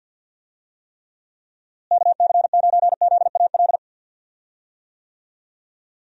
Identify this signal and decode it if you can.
Morse code, heard as a keyed tone, about 46 wpm, 700 Hz. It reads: KC9ZAL
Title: USA Call Signs - 6 Characters - Repeated 2x - 50wpm